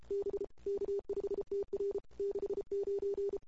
Morse code, heard as a keyed tone, about 28 wpm, 380 Hz. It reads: B K5TR 69
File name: CW / 2008-ss-cw-k5tr-op-n5rz -22:00